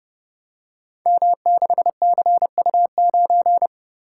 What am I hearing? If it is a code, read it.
Morse code, heard as a keyed tone, about 30 wpm, 700 Hz. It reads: M6CU9
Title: Code Groups - 5 Letter:Numbers - 30wpm